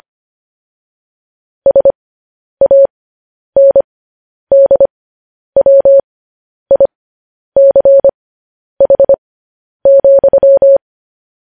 Morse code, heard as a keyed tone, about 25 wpm, spaced as for 10 wpm, 550 Hz. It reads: SANDWICH,
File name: podcast / News headlines-1x-25x10